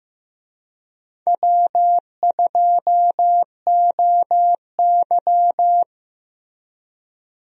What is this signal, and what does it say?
Morse code, heard as a keyed tone, about 15 wpm, 700 Hz. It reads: W2OY
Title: USA Call Signs - 4 Characters - Repeated 3x - 15wpm